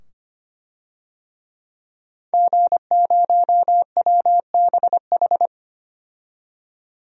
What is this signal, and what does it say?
Morse code, heard as a keyed tone, about 25 wpm, 700 Hz. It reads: G0WBH
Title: DXCC-100 Call Signs - 5 Characters - 25wpm